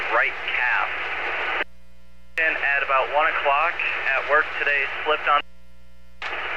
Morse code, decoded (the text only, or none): none